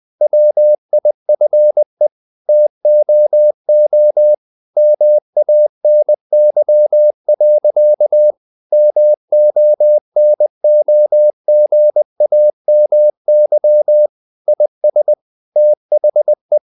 Morse code, heard as a keyed tone, about 20 wpm, 600 Hz. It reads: WIFE TOO MANY. MONOGAMY IS THE
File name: CW / Practice / EnglishQuotations0320